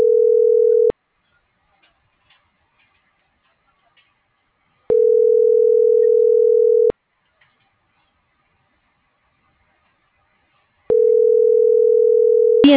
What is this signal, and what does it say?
Ringback tone (North America), 440 + 480 Hz.